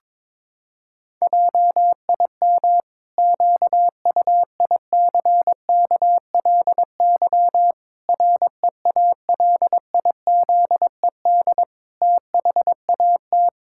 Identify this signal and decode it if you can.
Morse code, heard as a keyed tone, about 22 wpm, 700 Hz. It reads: JIM QUICKLY REALIZED THAT
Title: Pangram Challenge v2 - 22wpm